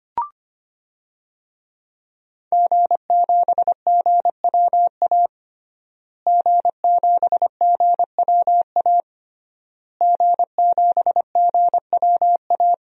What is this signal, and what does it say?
Morse code, heard as a keyed tone, about 25 wpm, 700 Hz. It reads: G7GWA G7GWA G7GWA
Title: DXCC-100 Call Signs - 5 Characters - Repeated 3x - 25wpm